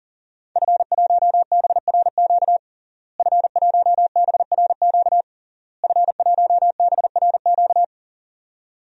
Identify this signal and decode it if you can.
Morse code, heard as a keyed tone, about 40 wpm, 700 Hz. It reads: F1BRQ F1BRQ F1BRQ
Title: DXCC-100 Call Signs - 5 Characters - Repeated 3x - 40wpm